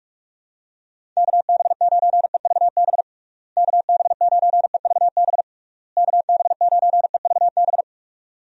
Morse code, heard as a keyed tone, about 46 wpm, 700 Hz. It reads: KB9EVB KB9EVB KB9EVB
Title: USA Call Signs - 6 Characters - Repeated 3x - 45wpm